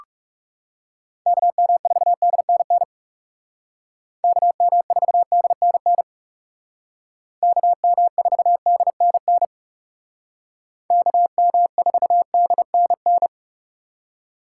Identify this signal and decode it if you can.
Morse code, heard as a keyed tone, about 39 wpm, 700 Hz. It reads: KM4DNN KM4DNN KM4DNN KM4DNN